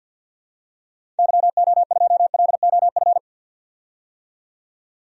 Morse code, heard as a keyed tone, about 49 wpm, 700 Hz. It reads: YO2LOF
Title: DXCC-100 Call Signs - 6 Characters - 50wpm